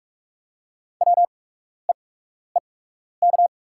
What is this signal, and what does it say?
Morse code, heard as a keyed tone, about 44 wpm, spaced as for 12 wpm, 690 Hz. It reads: WEEK
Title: 8x Character Spacing - Top 500 Words - 45wpm